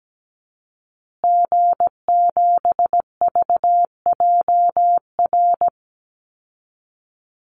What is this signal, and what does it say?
Morse code, heard as a keyed tone, about 17 wpm, 700 Hz. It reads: G7VJR
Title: Call Signs Repeated 2x - 17wpm